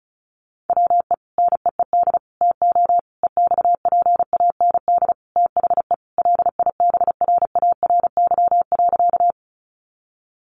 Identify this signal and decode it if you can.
Morse code, heard as a keyed tone, about 34 wpm, 700 Hz. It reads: WE NEED TO EXPAND THE LIBRARY.
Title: Sentences from Top 1400 Words 35wpm